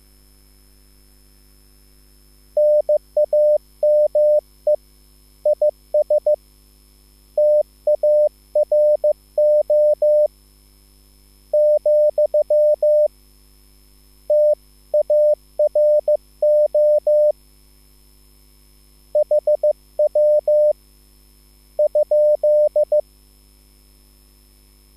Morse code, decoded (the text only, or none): NAME IS TARO , TARO HW ?